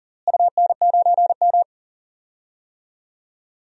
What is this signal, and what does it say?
Morse code, heard as a keyed tone, about 40 wpm, 700 Hz. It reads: UN9M
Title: DXCC-100 Call Signs - 4 Characters - Repeated 2x - 40wpm